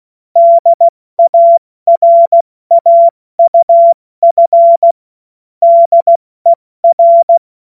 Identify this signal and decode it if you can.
Morse code, heard as a keyed tone, about 16 wpm, 680 Hz. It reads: DARAUF DER
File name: Witz-30-September-2025-70-F80